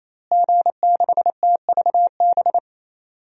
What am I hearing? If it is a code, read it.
Morse code, heard as a keyed tone, about 28 wpm, 700 Hz. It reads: G6TVB
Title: DXCC-100 Call Signs - 5 Characters - Repeated 3x - 28wpm